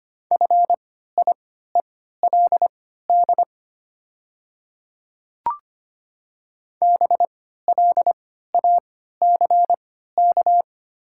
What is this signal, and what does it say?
Morse code, heard as a keyed tone, about 25 wpm, spaced as for 14 wpm, 700 Hz. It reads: FIELD BLACK
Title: 3x Character Spacing - Top 500 Words - 25wpm